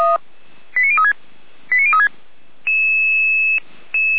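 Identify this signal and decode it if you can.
Five-tone selective call (ZVEI), heard as quick runs of five tones, tones of about 70 ms. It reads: 89036 89036
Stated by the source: ZVEI five-tone call received by SDR.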